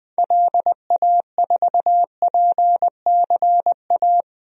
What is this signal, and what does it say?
Morse code, heard as a keyed tone, about 20 wpm, 700 Hz. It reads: LA4PCA
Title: DXCC-100 Call Signs - 6 Characters - 20wpm